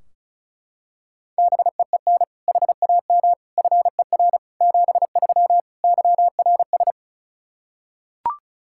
Morse code, heard as a keyed tone, about 34 wpm, 700 Hz. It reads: BEEN HAM FER 73 YRS
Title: Practice QSO - Exchange 3 - More Info - 35wpm